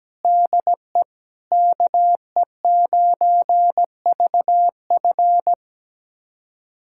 Morse code, heard as a keyed tone, about 17 wpm, 700 Hz. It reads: DE KE9VF
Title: USA Callsign DE Callsign - 5 Characters - 17wpm